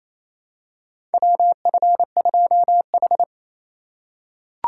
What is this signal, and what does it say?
Morse code, heard as a keyed tone, about 28 wpm, 700 Hz. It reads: WF2H
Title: USA Call Signs - 4 Characters - 3x Character Spacing - 28wpm